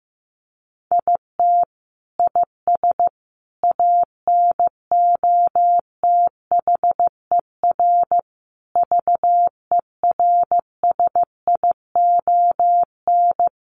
Morse code, heard as a keyed tone, about 15 wpm, 700 Hz. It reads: IT IS ANOTHER VERSION
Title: Sentences from Top 1400 Words 15wpm